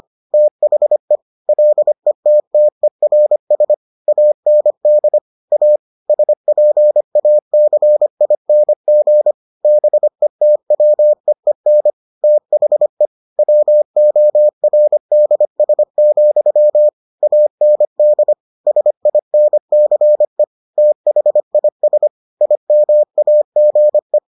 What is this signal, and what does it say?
Morse code, heard as a keyed tone, about 25 wpm, 600 Hz. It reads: THE LETTERS AND A SPACING BETWEEN THE WORDS, AND SINCE THIS IMAGE